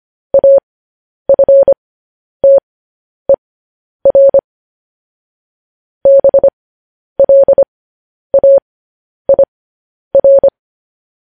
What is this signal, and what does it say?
Morse code, heard as a keyed tone, about 25 wpm, spaced as for 10 wpm, 550 Hz. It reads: AFTER BLAIR
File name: podcast / News headlines-1x-25x10